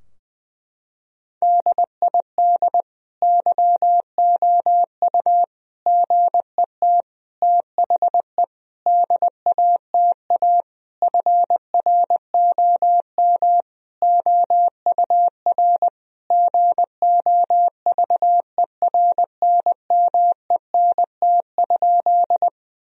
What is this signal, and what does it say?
Morse code, heard as a keyed tone, about 20 wpm, 700 Hz. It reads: DID YOU GET THE DATA FROM OUR GOVERNMENT?
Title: Sentences from Top 1000 Words - Review Part 3 - 20wpm